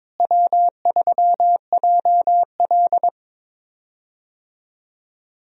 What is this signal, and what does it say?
Morse code, heard as a keyed tone, about 23 wpm, 700 Hz. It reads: W3JL
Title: USA Call Signs - 4 Characters - Repeated 3x - 22wpm